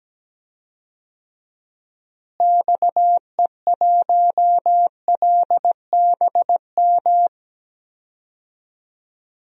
Morse code, heard as a keyed tone, about 17 wpm, 700 Hz. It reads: XE1LBM
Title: DXCC-100 Call Signs - 6 Characters - Repeated 2x - 17wpm